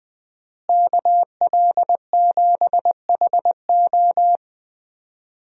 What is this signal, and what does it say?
Morse code, heard as a keyed tone, about 21 wpm, 700 Hz. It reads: KL7HO
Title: DXCC-100 Call Signs - 5 Characters - Repeated 2x - 20wpm